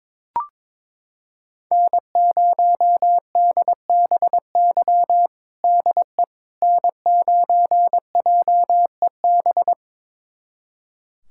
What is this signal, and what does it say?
Morse code, heard as a keyed tone, about 22 wpm, 700 Hz. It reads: N0DBY DE N9JEB
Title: USA Callsign DE Callsign 22wpm